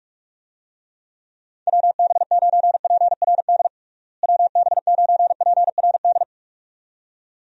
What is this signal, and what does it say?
Morse code, heard as a keyed tone, about 46 wpm, 700 Hz. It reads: WB9PRD WB9PRD
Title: USA Call Signs - 6 Characters - Repeated 2x - 45wpm